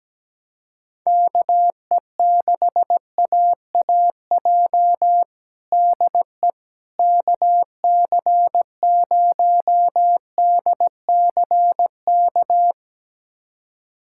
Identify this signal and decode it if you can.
Morse code, heard as a keyed tone, about 17 wpm, 700 Hz. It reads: KE6AAJ DE KC0DCK
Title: USA Callsign DE Callsign - 6 Characters - 17wpm